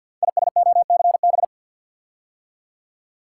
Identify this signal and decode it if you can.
Morse code, heard as a keyed tone, about 46 wpm, 700 Hz. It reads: ISOCB
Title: Code Groups - 5 Letters - 50wpm